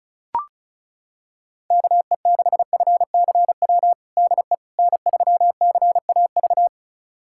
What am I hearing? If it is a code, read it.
Morse code, heard as a keyed tone, about 36 wpm, 700 Hz. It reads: KE6FCW DE N3CAV
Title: USA Callsign DE Callsign 35wpm